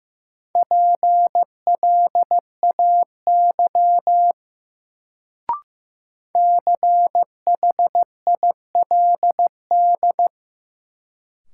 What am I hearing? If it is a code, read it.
Morse code, heard as a keyed tone, about 15 wpm, 700 Hz. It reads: PLAY CHILD
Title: Sets of 5 Words Encore - New Word Drill - 15wpm